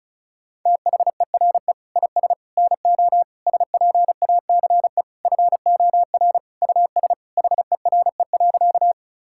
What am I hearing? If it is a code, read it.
Morse code, heard as a keyed tone, about 36 wpm, 700 Hz. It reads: THERE IS NO SPACE FOR US HERE.
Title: Sentences from Top 1000 Words - Review Part 5 - 35wpm